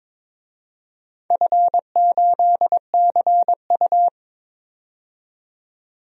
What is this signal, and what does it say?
Morse code, heard as a keyed tone, about 22 wpm, 700 Hz. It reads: F8CU
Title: DXCC-100 Call Signs - 4 Characters - 22wpm